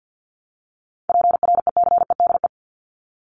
Morse code, heard as a keyed tone, about 49 wpm, 700 Hz. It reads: PRECEDE
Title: ING Suffix - Top 100 Words with Base Words 50wpm